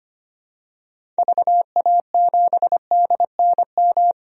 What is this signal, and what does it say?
Morse code, heard as a keyed tone, about 25 wpm, 700 Hz. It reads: VA7DNM
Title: DXCC-100 Call Signs - 6 Characters - 25wpm